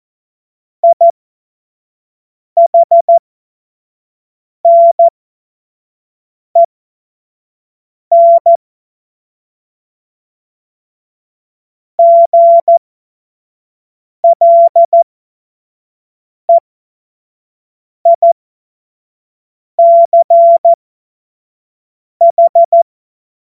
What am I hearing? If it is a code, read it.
Morse code, heard as a keyed tone, about 14 wpm, spaced as for 5 wpm, 680 Hz. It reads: IHNEN GLEICH
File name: Morsekurs / Witze / Witz 06 Februar 2024 25 F70